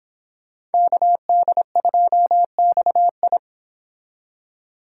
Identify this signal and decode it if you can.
Morse code, heard as a keyed tone, about 31 wpm, 690 Hz. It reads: KD2XI